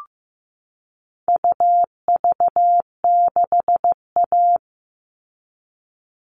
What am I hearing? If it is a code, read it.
Morse code, heard as a keyed tone, about 15 wpm, 700 Hz. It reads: UV6A
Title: All Cut Numbers - 4-Digit 15wpm